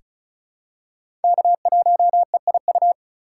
Morse code, heard as a keyed tone, about 34 wpm, 700 Hz. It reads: K1EIU
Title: USA Call Signs - 5 Characters - 4x Character Spacing - 35wpm